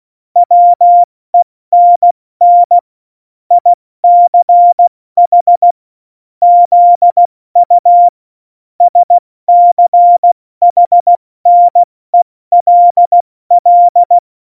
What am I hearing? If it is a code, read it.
Morse code, heard as a keyed tone, about 16 wpm, 700 Hz. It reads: WENN ICH ZU SCHNELL
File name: Morsekurs / Witze / Witz 05. Februar 2019, 70 F80